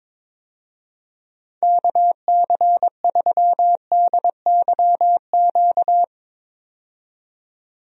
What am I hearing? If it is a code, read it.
Morse code, heard as a keyed tone, about 22 wpm, 700 Hz. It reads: KC3DYQ
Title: USA Call Signs - 6 Characters - 22wpm